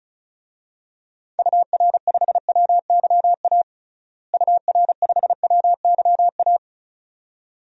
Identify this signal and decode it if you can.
Morse code, heard as a keyed tone, about 36 wpm, 700 Hz. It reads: UR5WYA UR5WYA
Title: DXCC-100 Call Signs - 6 Characters - Repeated 2x - 35wpm